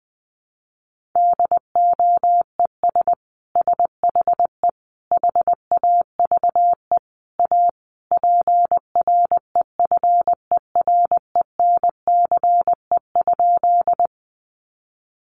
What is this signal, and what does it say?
Morse code, heard as a keyed tone, about 19 wpm, 700 Hz. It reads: DOES SHE HAVE A PREFERENCE?
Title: Sentences from Top 1500 Words 20wpm